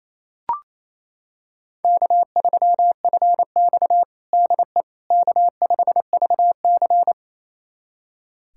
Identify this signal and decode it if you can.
Morse code, heard as a keyed tone, about 28 wpm, 700 Hz. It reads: K3FX DE K5VC
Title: USA Callsign DE Callsign - 4 Characters - 28wpm